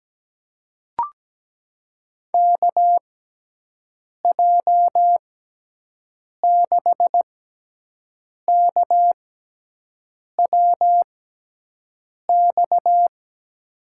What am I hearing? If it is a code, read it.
Morse code, heard as a keyed tone, about 17 wpm, spaced as for 6 wpm, 700 Hz. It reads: KJ6KWX
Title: USA Call Signs - 6 Characters - 6x Character Spacing - 17wpm